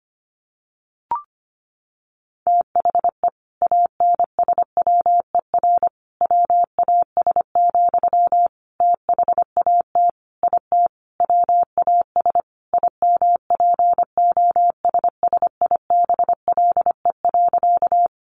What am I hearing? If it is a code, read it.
Morse code, heard as a keyed tone, about 25 wpm, 700 Hz. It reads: THE ANSWER WAS, THAT IT WAS IMPOSSIBLE.
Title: Sentences from Top 1100 Words 25wpm